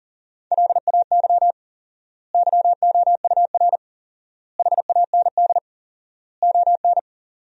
Morse code, heard as a keyed tone, about 40 wpm, 700 Hz. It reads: LAY YOUR HAND ON
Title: 4x Word-Spacing - Sentences from Top 400 Words - 40wpm